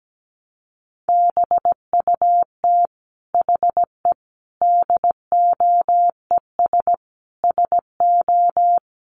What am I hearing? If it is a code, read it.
Morse code, heard as a keyed tone, about 17 wpm, 700 Hz. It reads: BUT HE DOES SO
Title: Sentences from Top 200 Words 17wpm